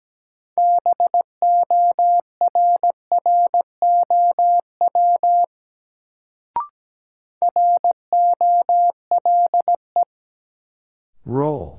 Morse code, heard as a keyed tone, about 17 wpm, 700 Hz. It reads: BORROW ROLE
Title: Sentences from Top 1500 Words - New Word Drill Alt - 17wpm